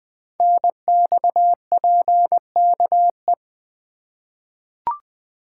Morse code, heard as a keyed tone, about 20 wpm, 700 Hz. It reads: NXPKE
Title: Code Groups - 5 Letters - 20wpm